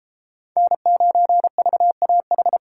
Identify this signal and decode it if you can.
Morse code, heard as a keyed tone, about 30 wpm, 700 Hz. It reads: N9VAH